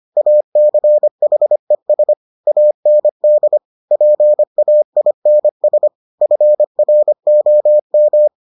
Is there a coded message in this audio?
Yes: Morse code from 0.2 to 8.3 s, heard as a keyed tone, about 25 wpm, 600 Hz. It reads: ACHES AND PAINS FROM